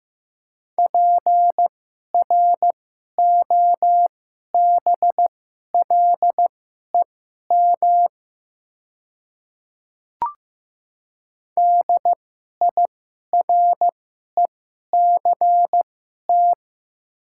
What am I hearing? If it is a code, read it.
Morse code, heard as a keyed tone, about 15 wpm, spaced as for 11 wpm, 700 Hz. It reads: PROBLEM DIRECT
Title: 2x Character Spacing - Top 500 Words - 15wpm